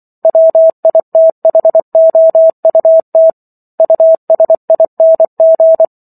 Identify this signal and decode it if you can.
Morse code, heard as a keyed tone, about 24 wpm, 650 Hz. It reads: WITHOUT USING